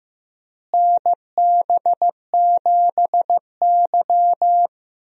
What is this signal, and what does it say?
Morse code, heard as a keyed tone, about 15 wpm, 700 Hz. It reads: NB7Y
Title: USA Call Signs - 4 Characters - Repeated 2x - 15wpm